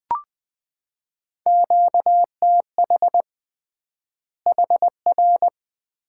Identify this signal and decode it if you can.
Morse code, heard as a keyed tone, about 20 wpm, 700 Hz. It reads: QTH HR
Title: Common QSO Phrases - 3x Word-Spacing - 20wpm